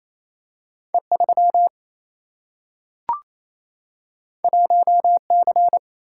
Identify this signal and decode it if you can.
Morse code, heard as a keyed tone, about 28 wpm, 700 Hz. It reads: E3 1C
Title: Single-Letter Single-Number 28wpm